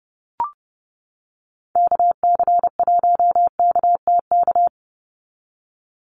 Morse code, heard as a keyed tone, about 30 wpm, 700 Hz. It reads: KC1KTK
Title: USA Call Signs 30wpm